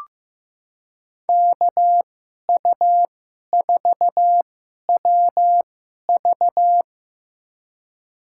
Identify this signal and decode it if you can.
Morse code, heard as a keyed tone, about 15 wpm, spaced as for 11 wpm, 700 Hz. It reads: KU4WV
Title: USA Call Signs - 5 Characters - 2x Character Spacing - 15wpm